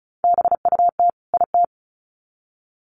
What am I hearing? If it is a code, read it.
Morse code, heard as a keyed tone, about 34 wpm, 700 Hz. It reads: BUT IT
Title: Sets of 2 Words 35wpm